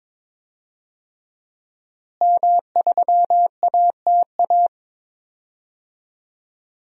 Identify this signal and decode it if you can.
Morse code, heard as a keyed tone, about 22 wpm, 700 Hz. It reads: M3ATA
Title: DXCC-100 Call Signs - 5 Characters - Repeated 3x - 22wpm